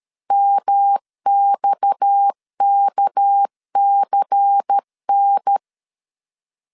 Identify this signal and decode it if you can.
Morse code, heard as a keyed tone, about 13 wpm, 800 Hz. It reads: MXKCN
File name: CW / DET / 1 9 1500C S A 12wpm 3800